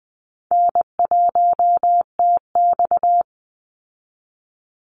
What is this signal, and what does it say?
Morse code, heard as a keyed tone, about 20 wpm, 700 Hz. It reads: N1TX
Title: Easy Call Signs Repeated 2x - 20wpm